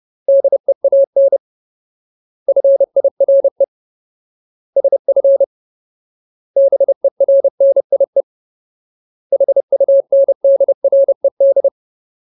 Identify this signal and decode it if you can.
Morse code, heard as a keyed tone, about 30 wpm, 550 Hz. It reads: DEAN FIRE SF BERNIE HUNDRED
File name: Ultimate Word List / 30WPM Ultimate